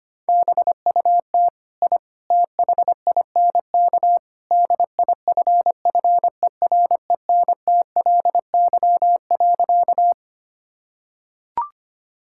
Morse code, heard as a keyed tone, about 25 wpm, 700 Hz. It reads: BUT I THINK DIFFERENTLY.
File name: practice / 25wpm / Sentences with 3-Word Phrases - 25wpm